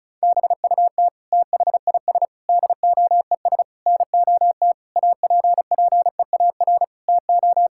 Morse code, heard as a keyed tone, about 36 wpm, 700 Hz. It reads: BUT THIS DOES NOT APPEAR TO
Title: Sentences from Top 1000 Words - Review Part 6 - 35wpm